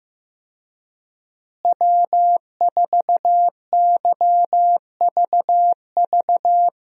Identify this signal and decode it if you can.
Morse code, heard as a keyed tone, about 15 wpm, 700 Hz. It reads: W4YVV
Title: USA Call Signs - 5 Characters - Repeated 2x - 15wpm